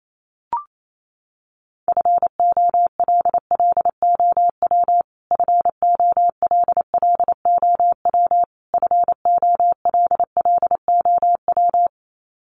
Morse code, heard as a keyed tone, about 28 wpm, 700 Hz. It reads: FOLLOW FOLLOW FOLLOW
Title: Top 200 Words Repeated 28wpm